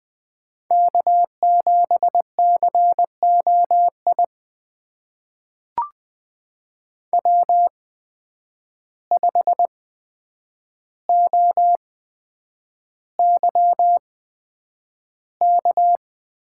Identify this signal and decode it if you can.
Morse code, heard as a keyed tone, about 20 wpm, spaced as for 9 wpm, 700 Hz. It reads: K7COI W5OYK
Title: USA Call Signs - 5 Characters - 8x Character Spacing - 20wpm